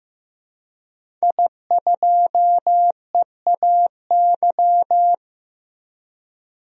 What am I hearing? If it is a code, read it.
Morse code, heard as a keyed tone, about 15 wpm, 700 Hz. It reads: I2EAY
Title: DXCC-100 Call Signs - 5 Characters - 15wpm